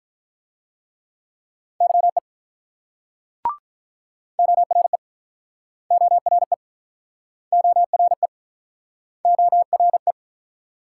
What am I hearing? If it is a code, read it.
Morse code, heard as a keyed tone, about 47 wpm, 700 Hz. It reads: YE ORE ORE ORE ORE